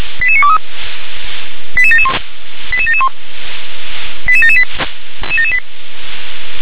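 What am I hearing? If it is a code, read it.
Five-tone selective call (ZVEI), heard as quick runs of five tones, tones of about 70 ms. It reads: 88923 88771 88771 88778
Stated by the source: ZVEI five-tone selective call received on SDR.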